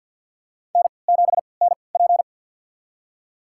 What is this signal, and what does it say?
Morse code, heard as a keyed tone, about 50 wpm, spaced as for 28 wpm, 700 Hz. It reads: N7NP
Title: USA Call Signs - 4 Characters - 3x Character Spacing - 50wpm